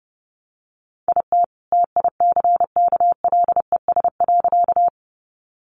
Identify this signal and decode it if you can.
Morse code, heard as a keyed tone, about 30 wpm, 700 Hz. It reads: IT TICKLES.
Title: Additional Easy Phrases 30wpm